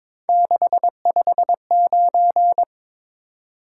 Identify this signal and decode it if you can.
Morse code, heard as a keyed tone, about 22 wpm, 700 Hz. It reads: 659
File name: practice / 22wpm / Three Letter-Number Combinations - 22wpm